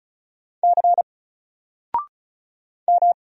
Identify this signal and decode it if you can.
Morse code, heard as a keyed tone, about 33 wpm, 700 Hz. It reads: C M
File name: Single-Letter-Number-35wpm